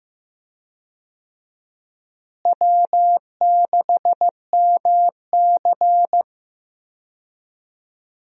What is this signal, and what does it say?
Morse code, heard as a keyed tone, about 15 wpm, 700 Hz. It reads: W6MC
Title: USA Call Signs - 4 Characters - Repeated 2x - 15wpm